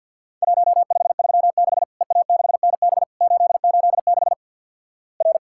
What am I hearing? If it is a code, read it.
Morse code, heard as a keyed tone, about 50 wpm, 700 Hz. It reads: 1536 EA6NB 886 R
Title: Worked All Europe DX Contest - QTC Exchange with Repeats - 50wpm